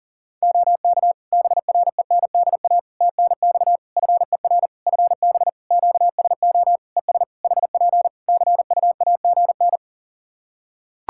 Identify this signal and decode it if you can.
Morse code, heard as a keyed tone, about 40 wpm, 700 Hz. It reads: OK BRENDA TNX FER FB QSO ES HP CUAGN
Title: Practice QSO - Exchange 4 - Ending - 40wpm